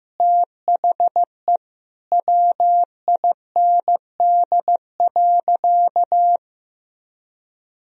Morse code, heard as a keyed tone, about 16 wpm, 700 Hz. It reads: THE WIND.
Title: Sentences from Top 1000 Words - Review Part 2 - 15wpm